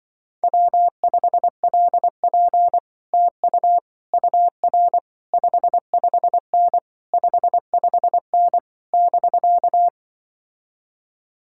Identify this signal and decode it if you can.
Morse code, heard as a keyed tone, about 24 wpm, 700 Hz. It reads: W5LP TU UR 55N 55N <BK>
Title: Speed-Racing Common QSO Phrases - 20wpm